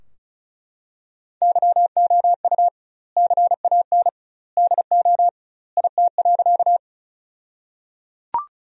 Morse code, heard as a keyed tone, about 34 wpm, 700 Hz. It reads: YOU CAN DO IT.
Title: 2x Word-Spacing - Sentences from Top 100 Words - 35wpm